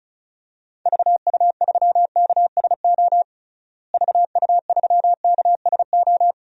Morse code, heard as a keyed tone, about 36 wpm, 700 Hz. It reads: VU3KSO VU3KSO
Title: DXCC-100 Call Signs - 6 Characters - Repeated 2x - 35wpm